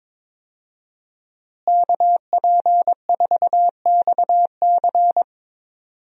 Morse code, heard as a keyed tone, about 22 wpm, 700 Hz. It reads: KP4XC
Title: DXCC-100 Call Signs - 5 Characters - Repeated 3x - 22wpm